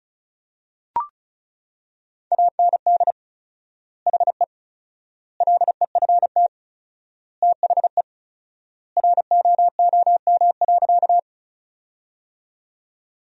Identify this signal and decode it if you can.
Morse code, heard as a keyed tone, about 36 wpm, 700 Hz. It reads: AND HE LEFT THE ROOM.
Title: 4x Word-Spacing - Sentences from Top 300 Words - 35wpm